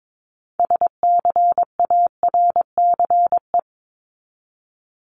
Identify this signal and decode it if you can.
Morse code, heard as a keyed tone, about 22 wpm, 700 Hz. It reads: SCARCE
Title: LY Suffix - Top 100 Words with Base Words 22wpm